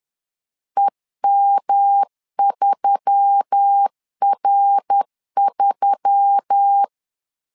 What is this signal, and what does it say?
Morse code, heard as a keyed tone, about 10 wpm, 800 Hz. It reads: EM3R3
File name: LEZIONI / 4 Q1S3 10wpm 2427